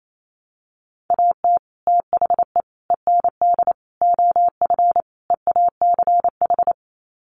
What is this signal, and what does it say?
Morse code, heard as a keyed tone, about 28 wpm, 700 Hz. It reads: AT THE END OF EACH
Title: Sets of 5 Words Encore 28wpm